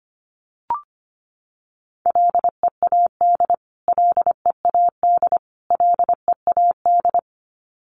Morse code, heard as a keyed tone, about 25 wpm, 700 Hz. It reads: LEAD LEAD LEAD
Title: Top 500 Words Repeated 25wpm